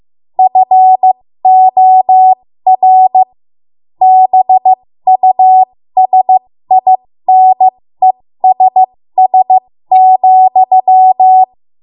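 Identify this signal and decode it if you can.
Morse code, heard as a keyed tone, about 15 wpm, spaced as for 13 wpm, 750 Hz. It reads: FOR BUSINESS,
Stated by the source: 13 wpm Morse code practice archive transmission.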